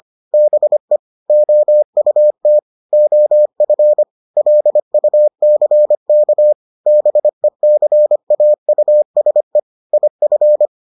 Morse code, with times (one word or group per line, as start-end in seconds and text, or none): 0.3-1.0 BE
1.3-2.6 OUT
2.9-4.0 OF
4.4-6.5 LUCK
6.9-9.6 BECAUSE
9.9-10.7 IF